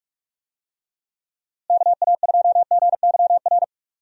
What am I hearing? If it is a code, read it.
Morse code, heard as a keyed tone, about 46 wpm, 700 Hz. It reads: KA2GYR
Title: USA Call Signs - 6 Characters - Repeated 3x - 45wpm